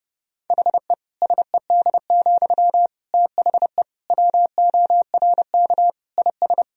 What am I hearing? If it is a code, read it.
Morse code, heard as a keyed tone, about 30 wpm, 700 Hz. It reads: HE SED, THE WORK IS
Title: Sentences from Top 400 Words and Abbreviations 30wpm